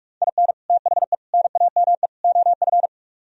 Morse code, heard as a keyed tone, about 46 wpm, 700 Hz. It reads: IN THE NAME OF